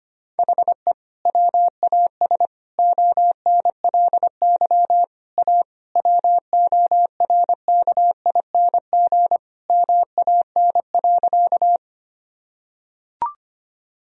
Morse code, heard as a keyed tone, about 25 wpm, 700 Hz. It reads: HE WAS ONLY A WORKING MAN.
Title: Sentences from Top 1000 Words - Review Part 4 - 25wpm